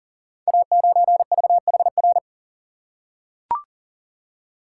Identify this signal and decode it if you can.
Morse code, heard as a keyed tone, about 40 wpm, 690 Hz. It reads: A9VHR
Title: Code Groups - 5 Letter:Numbers - 40wpm